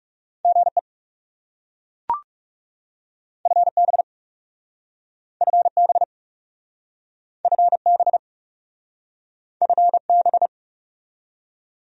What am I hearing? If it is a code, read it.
Morse code, heard as a keyed tone, about 37 wpm, 700 Hz. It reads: GE FB FB FB FB
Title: Speed-Racing Abbreviations - 30wpm